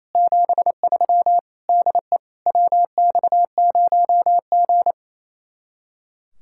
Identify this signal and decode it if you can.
Morse code, heard as a keyed tone, about 33 wpm, 700 Hz. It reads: 73 DE WX0G
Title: Speed-Racing Common QSO Phrases - 28wpm